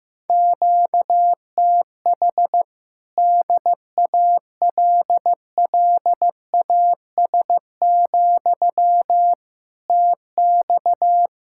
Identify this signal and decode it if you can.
Morse code, heard as a keyed tone, about 16 wpm, 700 Hz. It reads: QTH DALLAS, TX
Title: Speed-Racing Common QSO Phrases Encore - 15wpm